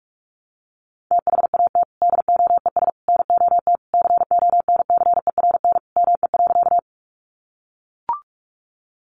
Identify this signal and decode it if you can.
Morse code, heard as a keyed tone, about 44 wpm, 700 Hz. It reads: THAT DOES NOT CONCERN ME.
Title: Sentences from Top 1000 Words 45wpm